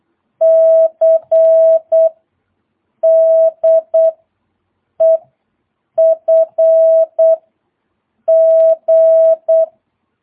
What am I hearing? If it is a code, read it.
Morse code, heard as a keyed tone, about 8 wpm, 650 Hz. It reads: C D E F G
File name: jh8cbh / 2022cw